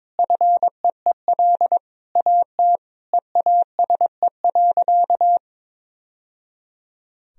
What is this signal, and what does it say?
Morse code, heard as a keyed tone, about 21 wpm, 700 Hz. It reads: FEEL AT EASE.
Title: Sentences from Top 1000 Words - Review Part 1 - 22wpm